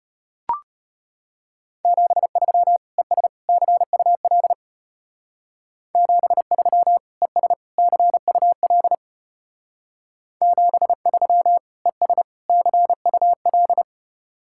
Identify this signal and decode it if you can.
Morse code, heard as a keyed tone, about 32 wpm, 700 Hz. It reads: 73 ES CUL 73 ES CUL 73 ES CUL